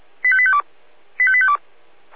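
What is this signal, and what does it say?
Five-tone selective call (ZVEI), heard as quick runs of five tones, tones of about 70 ms. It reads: 85852 85852
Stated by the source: ZVEI five-tone selective call received on SDR.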